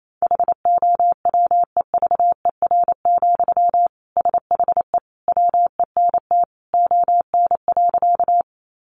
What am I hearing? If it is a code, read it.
Morse code, heard as a keyed tone, about 28 wpm, 700 Hz. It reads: HOWEVER, SHE WENT ON.
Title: Sentences from Top 600 Words 28wpm